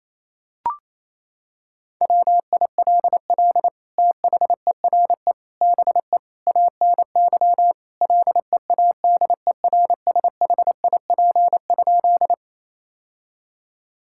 Morse code, heard as a keyed tone, about 28 wpm, 700 Hz. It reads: WILL THERE BE ANY LEADERSHIP?
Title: Sentences from Top 1600 Words 28wpm